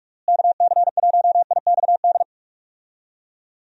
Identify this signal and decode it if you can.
Morse code, heard as a keyed tone, about 46 wpm, 700 Hz. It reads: KC1IXD
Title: USA Call Signs - 6 Characters - Repeated 3x - 45wpm